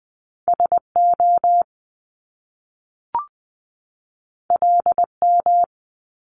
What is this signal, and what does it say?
Morse code, heard as a keyed tone, about 20 wpm, 700 Hz. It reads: S O L M
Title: Top Two Letter Combinations in Words 20wpm